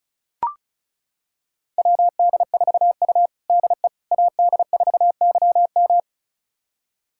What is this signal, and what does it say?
Morse code, heard as a keyed tone, about 36 wpm, 700 Hz. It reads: WD4U DE AD4YM
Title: USA Callsign DE Callsign 35wpm